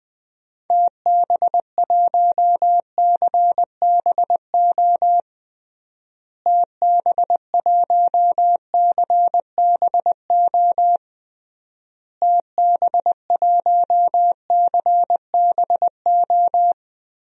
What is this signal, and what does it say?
Morse code, heard as a keyed tone, about 20 wpm, 700 Hz. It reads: TB1CBO TB1CBO TB1CBO
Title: DXCC-100 Call Signs - 6 Characters - Repeated 3x - 20wpm